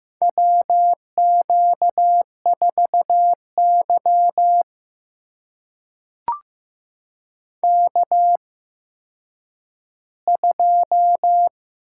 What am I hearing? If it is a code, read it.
Morse code, heard as a keyed tone, about 15 wpm, spaced as for 9 wpm, 700 Hz. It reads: WQ4Y K2
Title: USA Call Signs - 4 Characters - 8x Character Spacing - 15wpm